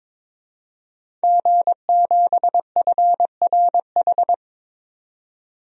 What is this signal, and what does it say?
Morse code, heard as a keyed tone, about 22 wpm, 700 Hz. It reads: G7FRH
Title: DXCC-100 Call Signs - 5 Characters - 22wpm